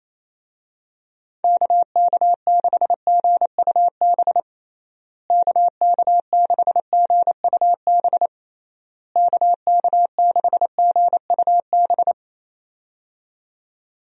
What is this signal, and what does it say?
Morse code, heard as a keyed tone, about 28 wpm, 700 Hz. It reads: KK6GUB KK6GUB KK6GUB
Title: USA Call Signs - 6 Characters - Repeated 3x - 28wpm